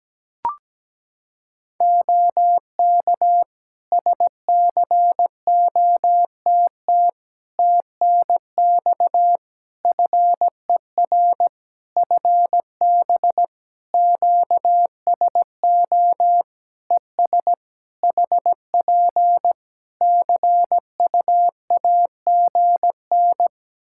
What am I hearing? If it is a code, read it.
Morse code, heard as a keyed tone, about 17 wpm, 700 Hz. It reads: OK SCOTT TNX FER FB QSO ES HP CUAGN
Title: Practice QSO - Exchange 4 - Ending - 17wpm